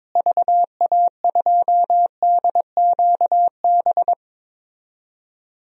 Morse code, heard as a keyed tone, about 22 wpm, 700 Hz. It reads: VA2DQB
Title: DXCC-100 Call Signs - 6 Characters - Repeated 3x - 22wpm